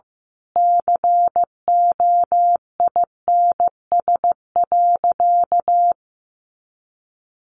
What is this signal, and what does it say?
Morse code, heard as a keyed tone, about 15 wpm, 700 Hz. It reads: COINS.
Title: A Sound Of Thunder 15wpm